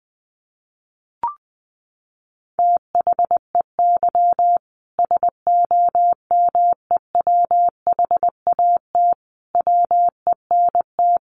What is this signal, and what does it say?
Morse code, heard as a keyed tone, about 20 wpm, 700 Hz. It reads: THEY SOMEWHAT WENT
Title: Sentences from Top 1300 Words 20wpm